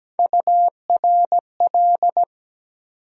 Morse code, heard as a keyed tone, about 17 wpm, 700 Hz. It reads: URL
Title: USA Call Sign Postfixes - 3 Letters 17wpm